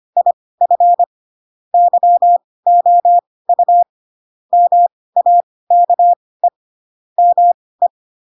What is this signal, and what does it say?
Morse code, heard as a keyed tone, about 25 wpm, spaced as for 18 wpm, 700 Hz. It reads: IF YOU MAKE ME
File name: Pinocchio / Pinocchio ch4 12 25 ik1hge